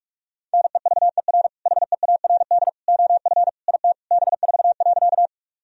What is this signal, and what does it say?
Morse code, heard as a keyed tone, about 47 wpm, 700 Hz. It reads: NEVER HEARD OF IT B4.
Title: Sentences from Top 400 Words and Abbreviations 45wpm